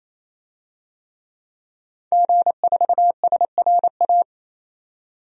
Morse code, heard as a keyed tone, about 28 wpm, 700 Hz. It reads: G4SRA
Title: DXCC-100 Call Signs - 5 Characters - 28wpm